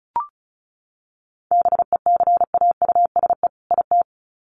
Morse code, heard as a keyed tone, about 34 wpm, 700 Hz. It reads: BECAUSE IT
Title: Sets of 2 Words 35wpm